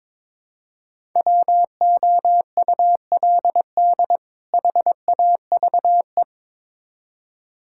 Morse code, heard as a keyed tone, about 22 wpm, 700 Hz. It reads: WOULD HAVE
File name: Build A Phrase - 2-Word Phrases - 22wpm